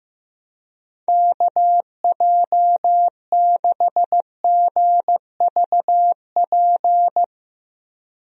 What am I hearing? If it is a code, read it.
Morse code, heard as a keyed tone, about 15 wpm, 700 Hz. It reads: KJ6GVP
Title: USA Call Signs - 6 Characters - Repeated 3x - 15wpm